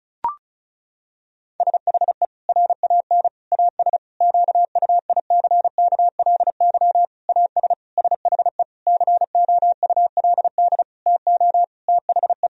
Morse code, heard as a keyed tone, about 36 wpm, 700 Hz. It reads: SHE RAN AS QUICKLY AS SHE COULD TO THE
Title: Sentences from Top 1000 Words - Review Part 4 - 35wpm